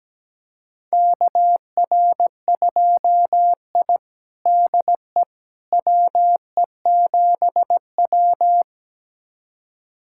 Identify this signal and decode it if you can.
Morse code, heard as a keyed tone, about 17 wpm, 700 Hz. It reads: KR2I DE WE7W
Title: USA Callsign DE Callsign - 4 Characters - 17wpm